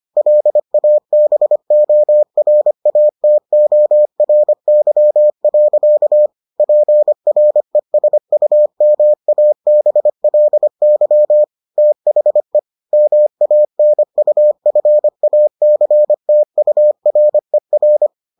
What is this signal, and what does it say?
Morse code, heard as a keyed tone, about 25 wpm, 600 Hz. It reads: LABORATORY. PRESUMABLY THE MANUFACTURER